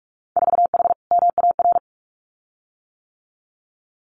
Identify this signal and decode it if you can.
Morse code, heard as a keyed tone, about 44 wpm, 700 Hz. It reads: 4H MAR
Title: Winter Field Day Exchanges Encore 45wpm